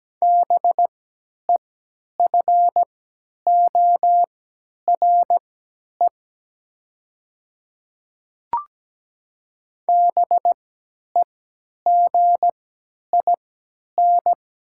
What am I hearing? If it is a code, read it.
Morse code, heard as a keyed tone, about 17 wpm, spaced as for 10 wpm, 700 Hz. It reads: BEFORE BEGIN
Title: 3x Character Spacing - Top 300 Words - 17wpm